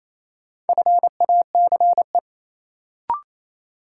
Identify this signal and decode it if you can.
Morse code, heard as a keyed tone, about 28 wpm, 700 Hz. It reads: FACE
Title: Sets of 3 Words - New Word Drill - 28wpm